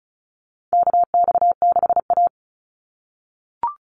Morse code, heard as a keyed tone, about 34 wpm, 700 Hz. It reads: KX6A
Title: NA SOTA Chaser Callsigns 35wpm